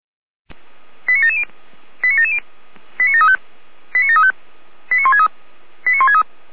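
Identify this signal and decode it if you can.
Five-tone selective call (ZVEI), heard as quick runs of five tones, tones of about 80 ms. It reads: 89770 89770 89735 89735 89163 89163
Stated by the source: ZVEI five-tone call received on SDR.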